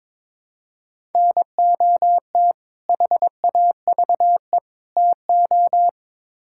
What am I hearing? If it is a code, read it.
Morse code, heard as a keyed tone, about 22 wpm, 700 Hz. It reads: NOT HAVE TO
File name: practice / 22wpm / Build A Phrase - 5-Word Phrases - 22wpm